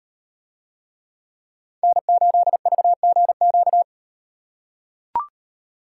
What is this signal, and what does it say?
Morse code, heard as a keyed tone, about 35 wpm, 700 Hz. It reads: N8VGQ